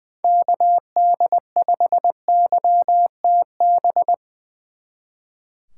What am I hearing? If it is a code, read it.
Morse code, heard as a keyed tone, about 22 wpm, 700 Hz. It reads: KD5YTB
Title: Speed Racing - USA 2x3 Call Signs 20wpm